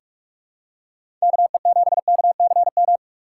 Morse code, heard as a keyed tone, about 44 wpm, 700 Hz. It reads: KE7KCM
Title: USA Call Signs - 2x Character Spacing - 45wpm